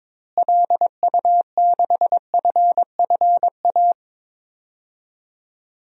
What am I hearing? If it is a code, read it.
Morse code, heard as a keyed tone, about 22 wpm, 700 Hz. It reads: LU6FFA
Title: DXCC-100 Call Signs - 6 Characters - Repeated 3x - 22wpm